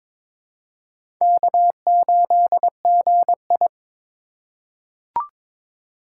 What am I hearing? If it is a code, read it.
Morse code, heard as a keyed tone, about 22 wpm, 700 Hz. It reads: K8GI
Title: USA Call Signs - 4 Characters - 3x Character Spacing - 22wpm